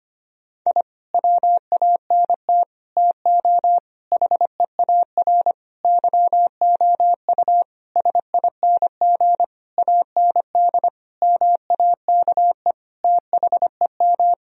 Morse code, heard as a keyed tone, about 25 wpm, 700 Hz. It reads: I WANT TO HEAR YOU SING AND MAKE THEM
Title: Sentences from Top 1000 Words - Review Part 4 - 25wpm